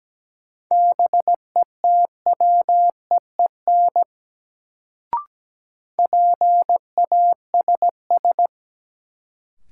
Morse code, heard as a keyed tone, about 17 wpm, 700 Hz. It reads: BETWEEN PASS
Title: Sentences from Top 400 Words - New Word Drill - 17wpm